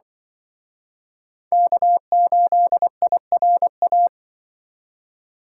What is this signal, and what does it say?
Morse code, heard as a keyed tone, about 24 wpm, 700 Hz. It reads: K8IRA
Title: Speed Racing - USA 1x3 Call Signs 20wpm